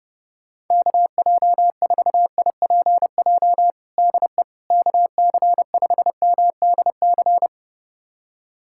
Morse code, heard as a keyed tone, about 31 wpm, 700 Hz. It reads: KJ4IPJ DE KC5MDC
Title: USA Callsign DE Callsign - 6 Characters - 30wpm